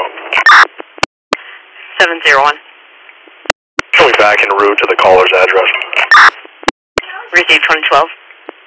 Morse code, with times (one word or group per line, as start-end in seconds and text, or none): none